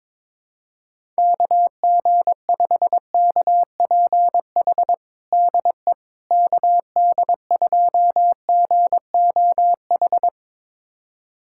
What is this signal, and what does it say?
Morse code, heard as a keyed tone, about 22 wpm, 700 Hz. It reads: KG5KPH DE KD2GOH
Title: USA Callsign DE Callsign 22wpm